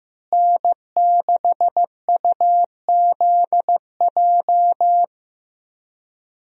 Morse code, heard as a keyed tone, about 15 wpm, 700 Hz. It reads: N6UZJ
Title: USA Call Signs - 5 Characters - Repeated 3x - 15wpm